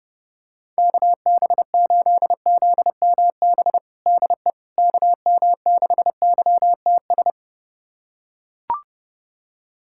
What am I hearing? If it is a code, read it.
Morse code, heard as a keyed tone, about 30 wpm, 700 Hz. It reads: KB8ZMB DE KM6YTS
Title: USA Callsign DE Callsign - 6 Characters - 30wpm